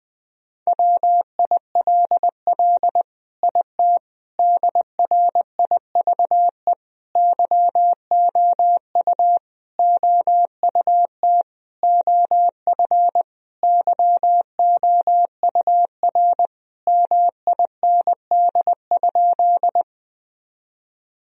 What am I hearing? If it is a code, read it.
Morse code, heard as a keyed tone, about 21 wpm, 700 Hz. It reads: WILL IT DRIVE YOU OUT OF YOUR MIND?
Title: Sentences from Top 1000 Words - Review Part 2 - 20wpm